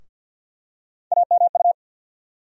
Morse code, heard as a keyed tone, about 53 wpm, 700 Hz. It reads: AMU
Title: USA Call Sign Postfixes - 3 Letters 50wpm